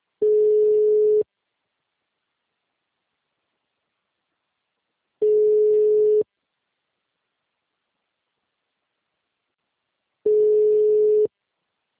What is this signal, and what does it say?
Ringback tone (Europe, 425 Hz), with speech.